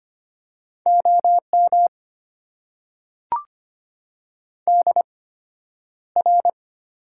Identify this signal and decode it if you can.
Morse code, heard as a keyed tone, about 25 wpm, spaced as for 11 wpm, 700 Hz. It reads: OM DR
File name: Top 90 QSO Elements - 8x Character Spacing - 25wpm